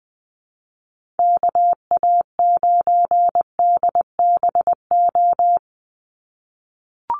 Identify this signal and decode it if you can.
Morse code, heard as a keyed tone, about 20 wpm, 700 Hz. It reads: KA9DBO
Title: USA Call Signs 20wpm